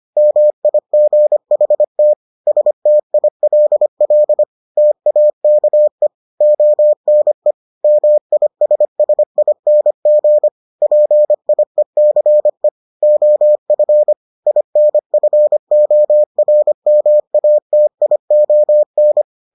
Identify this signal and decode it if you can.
Morse code, heard as a keyed tone, about 25 wpm, 600 Hz. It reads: MIGHT STILL TAKE ONE MISSING PIECE OF INFORMATION